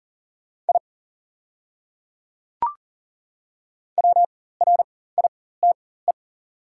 Morse code, heard as a keyed tone, about 40 wpm, spaced as for 19 wpm, 700 Hz. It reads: I WRITE
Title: 4x Character Spacing - Top 100 Words - 40wpm